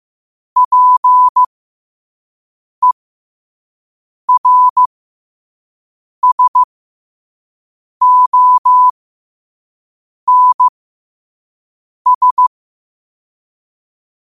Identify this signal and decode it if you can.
Morse code, heard as a keyed tone, about 15 wpm, spaced as for 5 wpm, 1000 Hz. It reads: PERSONS
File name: qotdfiles / 5wpm